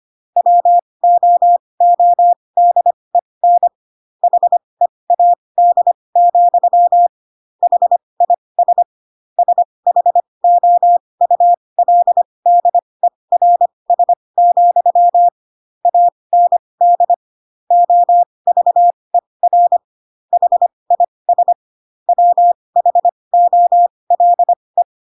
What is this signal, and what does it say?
Morse code, heard as a keyed tone, about 25 wpm, spaced as for 20 wpm, 700 Hz. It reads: WOODEN HEAD, HIS SHOULDERS, AND OVER HIS WHOLE
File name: Pinocchio / Pinocchio ch6 15 25 ik1hge